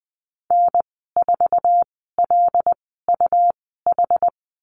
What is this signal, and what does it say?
Morse code, heard as a keyed tone, about 19 wpm, spaced as for 14 wpm, 700 Hz. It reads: N4LUH
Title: 147 Lesson 24g - Letter y - Call Signs 20wpm